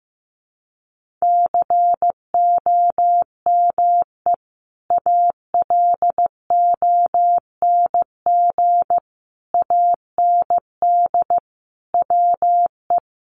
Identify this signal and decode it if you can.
Morse code, heard as a keyed tone, about 15 wpm, 700 Hz. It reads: COME ALONG AND WE
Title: Sentences from Top 1500 Words 15wpm